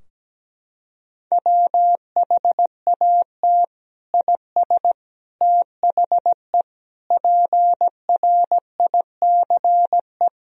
Morse code, heard as a keyed tone, about 17 wpm, 700 Hz. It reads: WHAT IS THE PRICE
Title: Sentences from Top 1000 Words - Review Part 4 - 17wpm